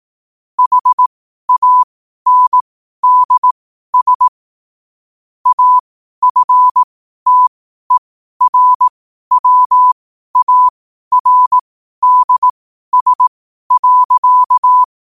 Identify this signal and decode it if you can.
Morse code, heard as a keyed tone, about 18 wpm, spaced as for 12 wpm, 1000 Hz. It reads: HANDS AFTERWARDS.
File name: qotdfiles / 12wpm